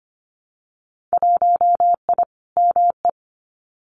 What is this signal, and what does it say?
Morse code, heard as a keyed tone, about 25 wpm, 700 Hz. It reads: 1I ME
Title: Winter Field Day Exchanges 25wpm